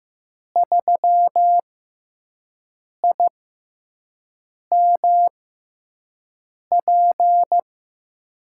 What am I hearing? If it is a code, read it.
Morse code, heard as a keyed tone, about 15 wpm, spaced as for 5 wpm, 700 Hz. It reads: 3IMP
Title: USA Call Signs - 6 Characters - 6x Character Spacing - 15wpm